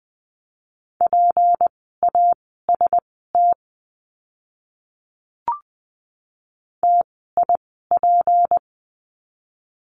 Morse code, heard as a keyed tone, about 20 wpm, spaced as for 15 wpm, 700 Hz. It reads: PAST TIP
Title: 153 Lesson 25f - Letter p - Words 20wpm